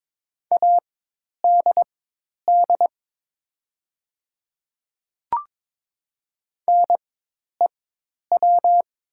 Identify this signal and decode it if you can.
Morse code, heard as a keyed tone, about 22 wpm, spaced as for 10 wpm, 700 Hz. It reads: ADD NEW
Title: 4x Character Spacing - Top 300 Words - 22wpm